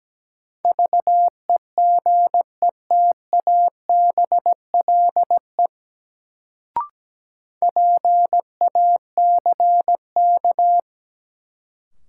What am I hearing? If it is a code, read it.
Morse code, heard as a keyed tone, about 17 wpm, 700 Hz. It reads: VEGETABLE PACK
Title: Sentences from Top 1300 Words - New Word Drill Alt - 17wpm